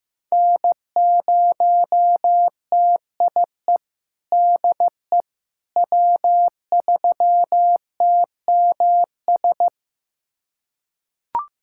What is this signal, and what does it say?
Morse code, heard as a keyed tone, about 15 wpm, 700 Hz. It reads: N0TIE DE W3TMS
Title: USA Callsign DE Callsign - 5 Characters - 15wpm